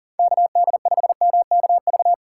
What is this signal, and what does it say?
Morse code, heard as a keyed tone, about 40 wpm, 700 Hz. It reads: KD5MKV
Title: USA Call Signs - 6 Characters - 6x Character Spacing - 40wpm